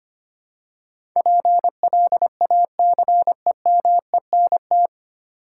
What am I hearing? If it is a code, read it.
Morse code, heard as a keyed tone, about 25 wpm, 700 Hz. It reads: PLACEMENT
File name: 9 Letter Words - First 3 Letters Given - 25wpm